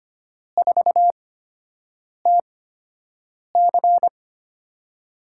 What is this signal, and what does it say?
Morse code, heard as a keyed tone, about 25 wpm, spaced as for 7 wpm, 700 Hz. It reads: 4TC
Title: USA Call Signs - 4 Characters - 8x Character Spacing - 25wpm